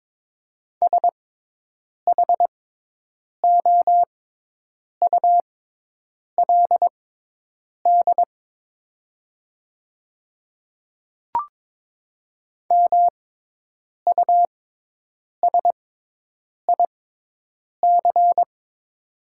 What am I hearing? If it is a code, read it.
Morse code, heard as a keyed tone, about 22 wpm, spaced as for 8 wpm, 700 Hz. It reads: SHOULD MUSIC
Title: 6x Character Spacing - Top 500 Words - 22wpm